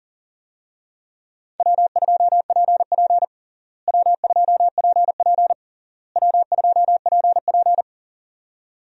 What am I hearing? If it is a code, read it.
Morse code, heard as a keyed tone, about 40 wpm, 700 Hz. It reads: W2PP W2PP W2PP
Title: USA Call Signs - 4 Characters - Repeated 3x - 40wpm